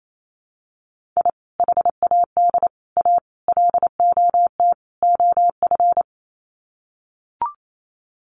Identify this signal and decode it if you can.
Morse code, heard as a keyed tone, about 28 wpm, 700 Hz. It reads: I HAD A LOT OF
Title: Sets of 5 Words Encore 28wpm